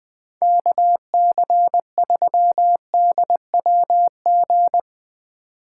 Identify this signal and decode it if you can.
Morse code, heard as a keyed tone, about 20 wpm, 700 Hz. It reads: KC3DWG
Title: USA Call Signs - 4x Character Spacing - 20wpm